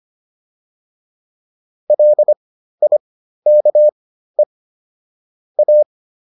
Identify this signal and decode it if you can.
Morse code, heard as a keyed tone, about 25 wpm, spaced as for 13 wpm, 600 Hz. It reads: LIKE A